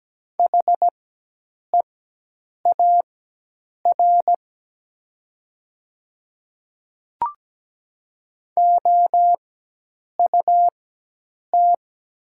Morse code, heard as a keyed tone, about 17 wpm, spaced as for 8 wpm, 690 Hz. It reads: HEAR OUT
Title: 4x Character Spacing - Top 300 Words - 17wpm